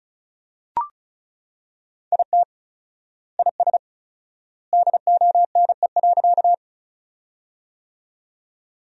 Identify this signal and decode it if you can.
Morse code, heard as a keyed tone, about 34 wpm, 700 Hz. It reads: IT IS DONE.
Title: 4x Word-Spacing - Sentences from Top 500 Words - 35wpm